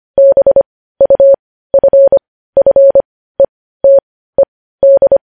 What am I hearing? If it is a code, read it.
Morse code, heard as a keyed tone, about 25 wpm, spaced as for 15 wpm, 550 Hz. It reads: BUFFETED
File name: podcast / News headlines-1x-25x15